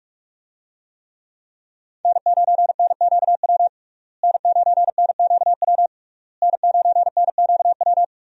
Morse code, heard as a keyed tone, about 42 wpm, 700 Hz. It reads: N9NQW N9NQW N9NQW
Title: USA Call Signs - 5 Characters - Repeated 3x - 45wpm